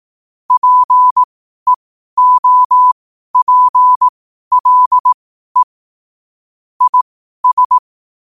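Morse code, heard as a keyed tone, about 18 wpm, spaced as for 12 wpm, 1000 Hz. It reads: PEOPLE IS
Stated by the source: 12 wpm Morse.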